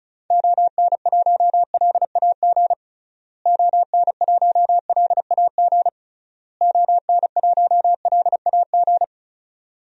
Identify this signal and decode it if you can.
Morse code, heard as a keyed tone, about 34 wpm, 700 Hz. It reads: ON1LAG ON1LAG ON1LAG
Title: DXCC-100 Call Signs - 6 Characters - Repeated 3x - 35wpm